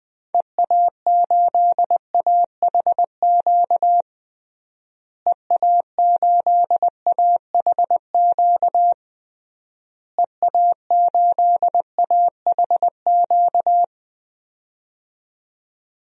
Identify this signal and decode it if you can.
Morse code, heard as a keyed tone, about 20 wpm, 700 Hz. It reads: EA8AHQ EA8AHQ EA8AHQ
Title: DXCC-100 Call Signs - 6 Characters - Repeated 3x - 20wpm